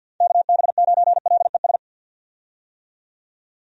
Morse code, heard as a keyed tone, about 50 wpm, 700 Hz. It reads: KB9LES
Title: USA Call Signs - 6 Characters - 50wpm